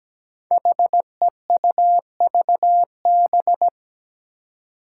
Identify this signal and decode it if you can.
Morse code, heard as a keyed tone, about 17 wpm, 700 Hz. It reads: HEUVB
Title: Five Challenging Letter-Number Combinations 17wpm